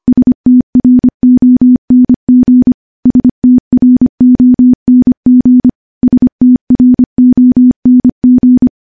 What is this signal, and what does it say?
Morse code, heard as a keyed tone, about 25 wpm, 250 Hz. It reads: STRONG STRONG STRONG
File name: morse-training-25WPM